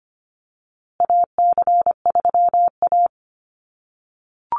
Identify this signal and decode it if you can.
Morse code, heard as a keyed tone, about 25 wpm, 700 Hz. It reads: AC3A
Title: USA Call Signs Repeated 3x 25wpm